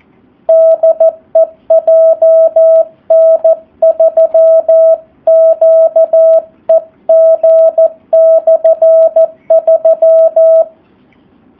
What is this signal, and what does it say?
Morse code, heard as a keyed tone, about 14 wpm, 650 Hz. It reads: DE JN3QEG/3
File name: jh8cbh / 2019cw / SP04- w